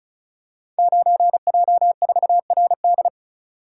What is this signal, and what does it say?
Morse code, heard as a keyed tone, about 36 wpm, 700 Hz. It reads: 9J4RD
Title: Code Groups - 5 Letter:Numbers - 35wpm